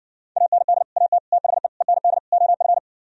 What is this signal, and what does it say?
Morse code, heard as a keyed tone, about 70 wpm, 700 Hz. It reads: AND AT THE END OF